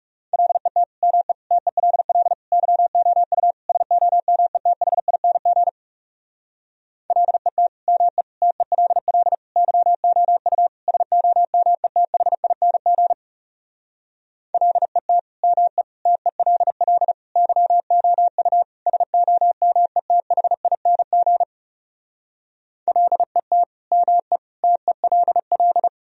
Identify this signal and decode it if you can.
Morse code, heard as a keyed tone, about 40 wpm, 700 Hz. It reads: LET ME TELL YOU SOMETHING LET ME TELL YOU SOMETHING LET ME TELL YOU SOMETHING LET ME TELL